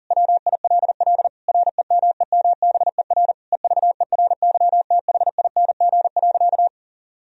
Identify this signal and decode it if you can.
Morse code, heard as a keyed tone, about 37 wpm, 700 Hz. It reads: WILL REMEMBER EVERYTHING.
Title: Sentences from Top 1000 Words - Review Part 1 - 40wpm